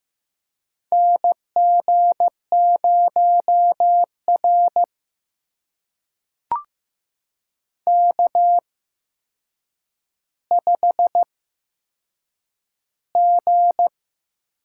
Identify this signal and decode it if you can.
Morse code, heard as a keyed tone, about 15 wpm, spaced as for 7 wpm, 700 Hz. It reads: NG0R K5G
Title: USA Call Signs - 4 Characters - 8x Character Spacing - 15wpm